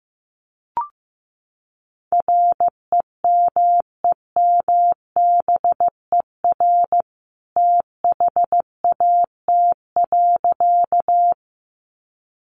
Morse code, heard as a keyed tone, about 15 wpm, 700 Hz. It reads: REMEMBER THAT.
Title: A Sound Of Thunder 15wpm